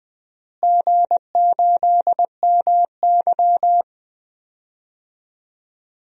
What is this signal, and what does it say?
Morse code, heard as a keyed tone, about 21 wpm, 700 Hz. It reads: G8MY
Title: DXCC-100 Call Signs - 4 Characters - Repeated 3x - 20wpm